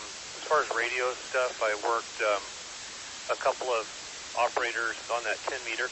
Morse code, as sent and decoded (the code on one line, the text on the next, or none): none